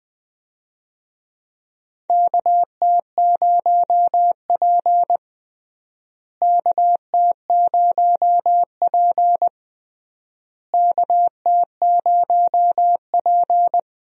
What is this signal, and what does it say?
Morse code, heard as a keyed tone, about 19 wpm, 700 Hz. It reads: KT0P KT0P KT0P
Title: USA Call Signs - 4 Characters - Repeated 3x - 20wpm